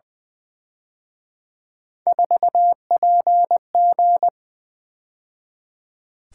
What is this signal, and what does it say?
Morse code, heard as a keyed tone, about 23 wpm, 700 Hz. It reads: 4PG
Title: Speed-Racing - Three Letter-Number Combinations - 20wpm